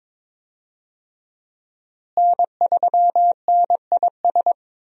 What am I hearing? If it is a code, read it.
Morse code, heard as a keyed tone, about 22 wpm, 700 Hz. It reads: N3NIS
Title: USA Call Signs - 5 Characters - Repeated 3x - 22wpm